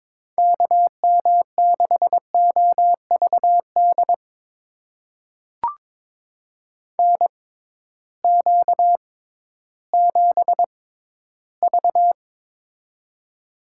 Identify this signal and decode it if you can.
Morse code, heard as a keyed tone, about 22 wpm, spaced as for 13 wpm, 700 Hz. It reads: KM6OVD NQ7V
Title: USA Call Signs - 6x Character Spacing - 22wpm